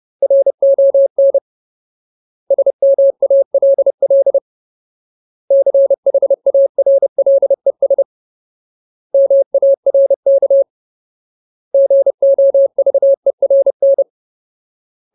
Morse code, heard as a keyed tone, about 30 wpm, 550 Hz. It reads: RON SMALL CHARLES MARK GOVERN
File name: Ultimate Word List / 30WPM Ultimate